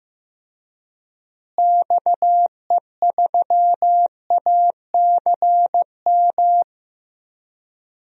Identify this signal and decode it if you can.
Morse code, heard as a keyed tone, about 16 wpm, 700 Hz. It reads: XE3ACM
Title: DXCC-100 Call Signs - 6 Characters - Repeated 3x - 15wpm